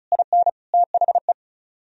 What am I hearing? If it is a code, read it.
Morse code, heard as a keyed tone, about 45 wpm, 700 Hz. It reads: IN THE